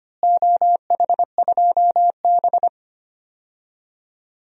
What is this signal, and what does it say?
Morse code, heard as a keyed tone, about 25 wpm, 700 Hz. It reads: OH2B
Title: DXCC-100 Call Signs - 4 Characters - Repeated 2x - 25wpm